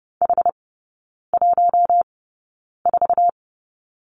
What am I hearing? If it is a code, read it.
Morse code, heard as a keyed tone, about 30 wpm, spaced as for 9 wpm, 700 Hz. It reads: H14
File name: course / 30:10wpm / 087 Lesson 15e - Number 5 - Three Characters 30wpm